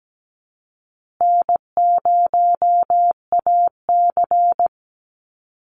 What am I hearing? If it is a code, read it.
Morse code, heard as a keyed tone, about 17 wpm, 700 Hz. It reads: N0AC
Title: Easy Call Signs Repeated 2x - 17wpm